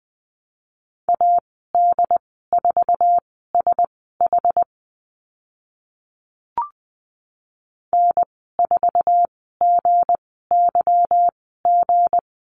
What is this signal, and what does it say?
Morse code, heard as a keyed tone, about 20 wpm, spaced as for 14 wpm, 700 Hz. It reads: AD4SH N4GYG
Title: 161 Lesson 26g - Letter g - Call Signs 20wpm